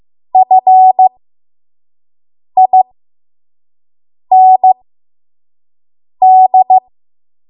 Morse code, heard as a keyed tone, about 15 wpm, spaced as for 5 wpm, 750 Hz. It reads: FIND
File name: Morse / 05WPM